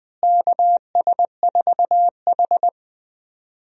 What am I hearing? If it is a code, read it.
Morse code, heard as a keyed tone, about 20 wpm, 700 Hz. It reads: KS4H
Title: USA Call Signs - 4 Characters - Repeated 3x - 20wpm